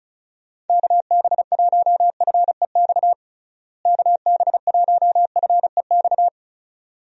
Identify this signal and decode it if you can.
Morse code, heard as a keyed tone, about 36 wpm, 700 Hz. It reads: KB1FEX KB1FEX
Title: USA Call Signs - 6 Characters - Repeated 2x - 35wpm